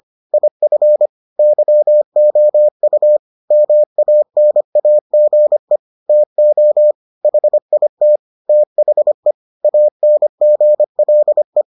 Morse code, heard as a keyed tone, about 25 wpm, 600 Hz. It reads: IF YOU MANAGE TO HIT THE ANGLE